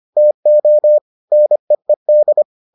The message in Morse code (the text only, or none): TO NEED